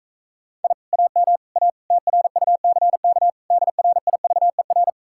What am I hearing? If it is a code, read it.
Morse code, heard as a keyed tone, about 45 wpm, 700 Hz. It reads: I AM A TRUCK DRIVER